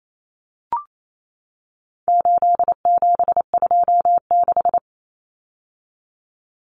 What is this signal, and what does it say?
Morse code, heard as a keyed tone, about 28 wpm, 700 Hz. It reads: 8726
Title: Common Cut Numbers - 4-Digit 28wpm